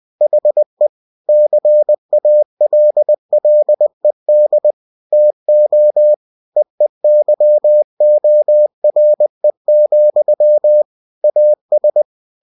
Morse code, heard as a keyed tone, about 20 wpm, 600 Hz. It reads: HE CALLED TO EEYORE, AS
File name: cw-books / winnie-the-pooh / winnie-20wpm-page-0048